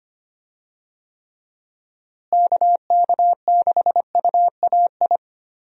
Morse code, heard as a keyed tone, about 25 wpm, 700 Hz. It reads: KK6UAI
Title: USA Call Signs - 6 Characters - Repeated 3x - 25wpm